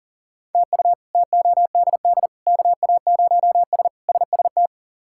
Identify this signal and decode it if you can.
Morse code, heard as a keyed tone, about 40 wpm, 700 Hz. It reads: TU TODD KA0S SST
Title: K1USN SST QSO Practice - 40wpm